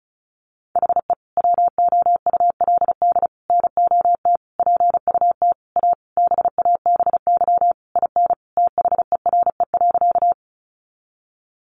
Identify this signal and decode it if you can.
Morse code, heard as a keyed tone, about 34 wpm, 700 Hz. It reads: HE WOULD NOT PUT A BABY IN THERE.
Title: Easy Phrases 35wpm